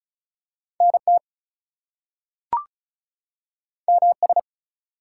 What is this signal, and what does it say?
Morse code, heard as a keyed tone, about 36 wpm, 700 Hz. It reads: NT MS
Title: US States and Canadian Provinces 35wpm